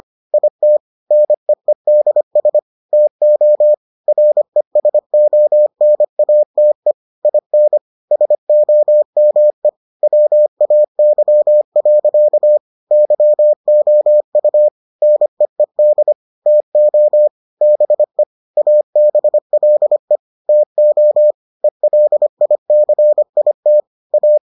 Morse code, heard as a keyed tone, about 25 wpm, 600 Hz. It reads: IT NEEDS TO RESONATE IN SOME WAY. YOU NEED TO BE ABLE TO ELICIT A